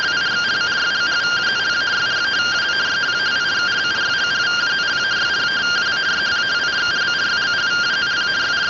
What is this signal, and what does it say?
Two-tone teleprinter signal, about 170 Hz shift.